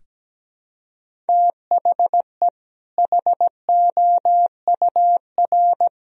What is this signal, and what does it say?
Morse code, heard as a keyed tone, about 17 wpm, 700 Hz. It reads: THE HOUR
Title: Sentences from Top 1000 Words - Review Part 1 - 17wpm